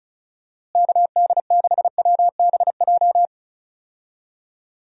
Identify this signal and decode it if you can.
Morse code, heard as a keyed tone, about 36 wpm, 700 Hz. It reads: KD6WBJ
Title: USA Call Signs - 6 Characters - Repeated 3x - 35wpm